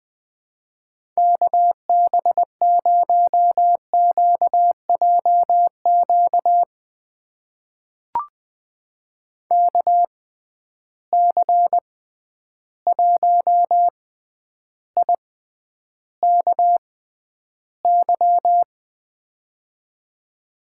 Morse code, heard as a keyed tone, about 21 wpm, spaced as for 10 wpm, 700 Hz. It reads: KB0QJQ KC1IKY
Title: USA Call Signs - 6x Character Spacing - 20wpm